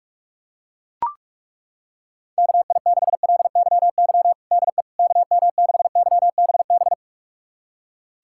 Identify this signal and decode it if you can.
Morse code, heard as a keyed tone, about 41 wpm, 700 Hz. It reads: KI6LYY DE KM6YBB
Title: USA Callsign DE Callsign - 6 Characters - 45wpm